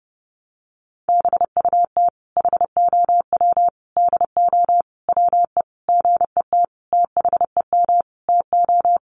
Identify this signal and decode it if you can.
Morse code, heard as a keyed tone, about 30 wpm, 700 Hz. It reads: BUT HOW DO WE GET THEM TO
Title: Sentences from Top 700 Words 30wpm